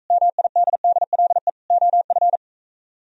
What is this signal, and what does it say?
Morse code, heard as a keyed tone, about 44 wpm, 700 Hz. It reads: MIDDLE OF